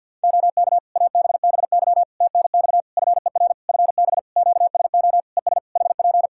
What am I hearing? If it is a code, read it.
Morse code, heard as a keyed tone, about 50 wpm, 700 Hz. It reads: OK ABBY TNX FER FB QSO ES HP
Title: Practice QSO - Exchange 4 - Ending - 50wpm